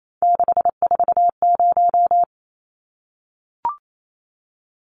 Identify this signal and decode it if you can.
Morse code, heard as a keyed tone, about 28 wpm, 700 Hz. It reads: 640
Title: Numbers - 3-Digit 28wpm